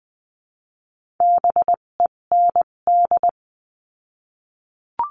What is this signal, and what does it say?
Morse code, heard as a keyed tone, about 20 wpm, spaced as for 17 wpm, 700 Hz. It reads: BEND
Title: 257 Lesson 42b - Words 14-20wpm